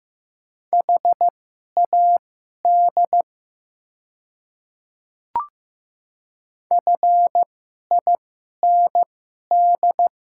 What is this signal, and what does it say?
Morse code, heard as a keyed tone, about 15 wpm, spaced as for 11 wpm, 690 Hz. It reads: HAD FIND
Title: 2x Character Spacing - Top 100 Words - 15wpm